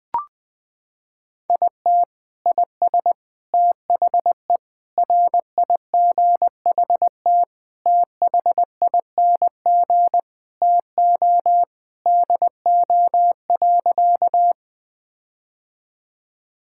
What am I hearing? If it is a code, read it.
Morse code, heard as a keyed tone, about 20 wpm, 700 Hz. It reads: IT IS THE RIGHT THING TO DO.
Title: Sentences with 5-Word Phrases - 20wpm